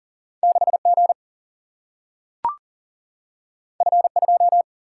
Morse code, heard as a keyed tone, about 40 wpm, 700 Hz. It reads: 6G F2
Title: Single-Letter Single-Number 40wpm